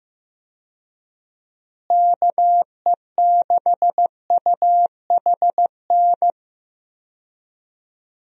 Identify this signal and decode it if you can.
Morse code, heard as a keyed tone, about 15 wpm, 700 Hz. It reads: KE6UHN
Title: USA Call Signs - 6 Characters - Repeated 3x - 15wpm